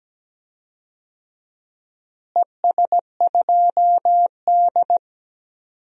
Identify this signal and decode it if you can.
Morse code, heard as a keyed tone, about 17 wpm, 700 Hz. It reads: ES2D
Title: DXCC-100 Call Signs - 4 Characters - Repeated 3x - 17wpm